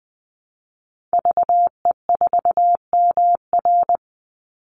Morse code, heard as a keyed tone, about 20 wpm, 700 Hz. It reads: VE4MR
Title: Call Signs - 20wpm